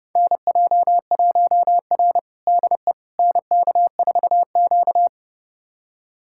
Morse code, heard as a keyed tone, about 30 wpm, 700 Hz. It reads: NJ1R DE NK4Q
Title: USA Callsign DE Callsign - 4 Characters - 30wpm